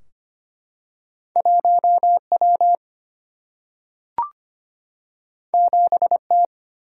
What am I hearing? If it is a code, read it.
Morse code, heard as a keyed tone, about 25 wpm, 700 Hz. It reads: 1W 7T
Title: Single-Letter Single-Number 25wpm